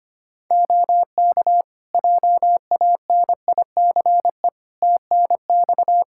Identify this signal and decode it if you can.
Morse code, heard as a keyed tone, about 25 wpm, 700 Hz. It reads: OK JANICE TNX
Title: K1USN SST QSO Practice 25wpm